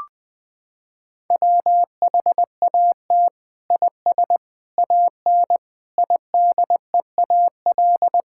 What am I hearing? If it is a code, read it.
Morse code, heard as a keyed tone, about 20 wpm, 700 Hz. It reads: WHAT IS AN IDEAL
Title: Sentences from Top 1600 Words 20wpm